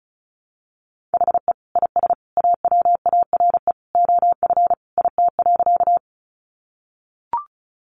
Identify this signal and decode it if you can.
Morse code, heard as a keyed tone, about 34 wpm, 700 Hz. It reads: HE IS AWARE OF IT.
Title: Sentences from Top 1200 Words 35wpm